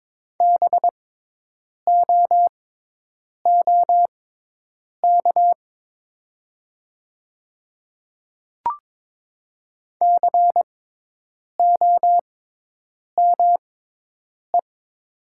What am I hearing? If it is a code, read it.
Morse code, heard as a keyed tone, about 22 wpm, spaced as for 8 wpm, 700 Hz. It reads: BOOK COME
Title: 6x Character Spacing - Top 500 Words - 22wpm